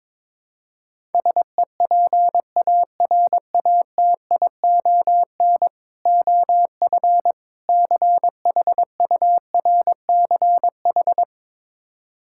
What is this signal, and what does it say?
Morse code, heard as a keyed tone, about 22 wpm, 700 Hz. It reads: SEPARATION OF CHURCH
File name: practice / 22wpm / Build A Phrase - 5-Word Phrases - 22wpm